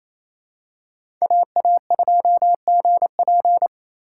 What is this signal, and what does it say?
Morse code, heard as a keyed tone, about 27 wpm, 700 Hz. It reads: AA2GP
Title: USA Call Signs - 5 Characters - 4x Character Spacing - 28wpm